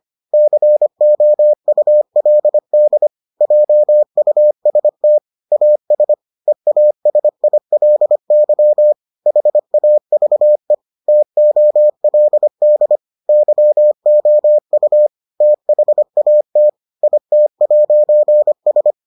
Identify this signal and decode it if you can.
Morse code, heard as a keyed tone, about 25 wpm, 600 Hz. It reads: COULD JUST AS EASILY HAVE TOLD YOU THAT IT'S